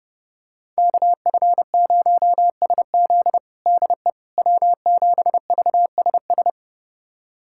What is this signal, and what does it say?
Morse code, heard as a keyed tone, about 30 wpm, 700 Hz. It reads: KF0SZ DE W7VSS
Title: USA Callsign DE Callsign - 5 Characters - 30wpm